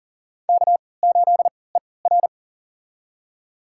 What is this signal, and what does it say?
Morse code, heard as a keyed tone, about 40 wpm, spaced as for 23 wpm, 700 Hz. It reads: K8ER
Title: USA Call Signs - 3x Character Spacing - 40wpm